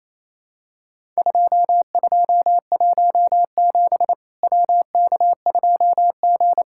Morse code, heard as a keyed tone, about 27 wpm, 700 Hz. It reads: 2217 WK2G
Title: Worked All Europe DX Contest - QTC Exchange - 28wpm